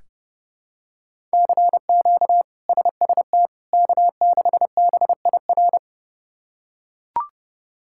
Morse code, heard as a keyed tone, about 30 wpm, 700 Hz. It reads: CQ SST K6BIR
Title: K1USN SST QSO Practice - 30wpm